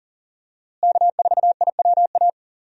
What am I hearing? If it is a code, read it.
Morse code, heard as a keyed tone, about 40 wpm, 700 Hz. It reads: K4IWA
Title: USA Call Signs - 5 Characters - Repeated 3x - 40wpm